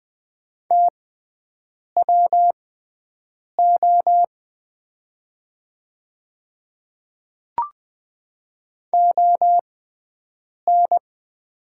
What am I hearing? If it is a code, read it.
Morse code, heard as a keyed tone, about 20 wpm, spaced as for 7 wpm, 690 Hz. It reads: TWO ON
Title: 6x Character Spacing - Top 100 Words - 20wpm